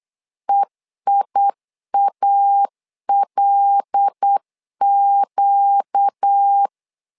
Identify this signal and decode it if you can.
Morse code, heard as a keyed tone, about 8 wpm, 800 Hz. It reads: EIALQ
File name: LEZIONI / 5 LBVC / 5 L 8wpm 2831